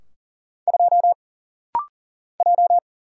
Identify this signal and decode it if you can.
Morse code, heard as a keyed tone, about 39 wpm, 700 Hz. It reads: 2 J
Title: Single Letter-Number Alt - Rapid-Fire 40wpm